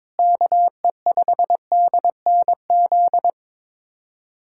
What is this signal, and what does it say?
Morse code, heard as a keyed tone, about 22 wpm, 700 Hz. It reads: KE5DNZ
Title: USA Callsign DE Callsign - 6 Characters - 22wpm